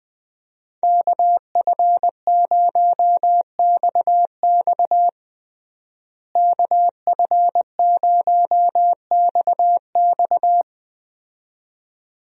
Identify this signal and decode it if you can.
Morse code, heard as a keyed tone, about 21 wpm, 700 Hz. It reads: KF0XX KF0XX
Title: USA Call Signs - 5 Characters - Repeated 2x - 20wpm